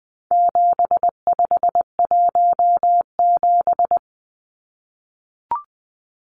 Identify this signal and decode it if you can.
Morse code, heard as a keyed tone, about 20 wpm, 700 Hz. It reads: 7517
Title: Numbers - 4-Digit 20wpm